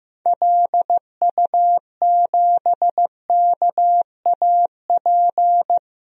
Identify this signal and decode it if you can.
Morse code, heard as a keyed tone, about 15 wpm, 700 Hz. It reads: LU7KAP
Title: DXCC-100 Call Signs - 6 Characters - 15wpm